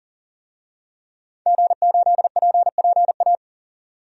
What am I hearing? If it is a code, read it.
Morse code, heard as a keyed tone, about 40 wpm, 700 Hz. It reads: G8PPA
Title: DXCC-100 Call Signs - 5 Characters - 40wpm